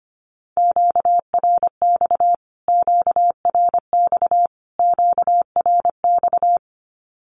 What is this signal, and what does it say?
Morse code, heard as a keyed tone, about 25 wpm, 700 Hz. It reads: QRX QRX QRX
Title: Q-Codes Repeated 3x 25wpm